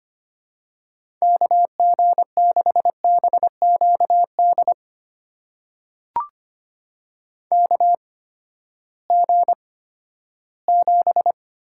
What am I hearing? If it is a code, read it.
Morse code, heard as a keyed tone, about 25 wpm, spaced as for 14 wpm, 700 Hz. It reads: KG6BQD KG7
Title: USA Call Signs - 6 Characters - 8x Character Spacing - 25wpm